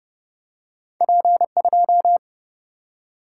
Morse code, heard as a keyed tone, about 30 wpm, 700 Hz. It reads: P 2
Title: Single-Letter Single-Number 30wpm